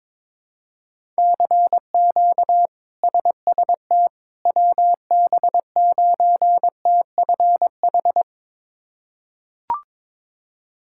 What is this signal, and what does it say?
Morse code, heard as a keyed tone, about 22 wpm, 700 Hz. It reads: CQ SST WB9TFH
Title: K1USN SST QSO Practice - 22wpm